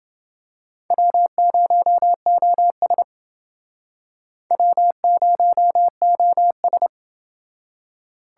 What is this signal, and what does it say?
Morse code, heard as a keyed tone, about 29 wpm, 700 Hz. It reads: W0OS W0OS